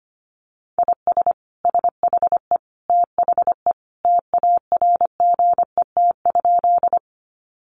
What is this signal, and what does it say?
Morse code, heard as a keyed tone, about 25 wpm, 700 Hz. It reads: IS SHE THE TARGET?
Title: Sentences from Top 1200 Words 25wpm